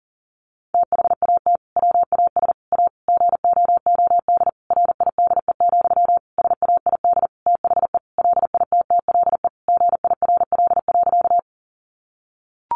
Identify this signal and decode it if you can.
Morse code, heard as a keyed tone, about 40 wpm, 700 Hz. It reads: THAT WAS A GOOD RIDE, SAID THE LITTLE GIRL.
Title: Sentences from Top 1200 Words 40wpm